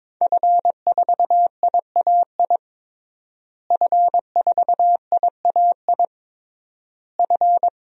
Morse code, heard as a keyed tone, about 23 wpm, 700 Hz. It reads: F4IAI F4IAI F
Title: DXCC-100 Call Signs - 5 Characters - Repeated 3x - 22wpm